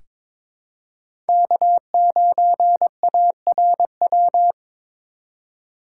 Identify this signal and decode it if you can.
Morse code, heard as a keyed tone, about 22 wpm, 700 Hz. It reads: K9ARW
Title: USA Call Signs - 5 Characters - 2x Character Spacing - 22wpm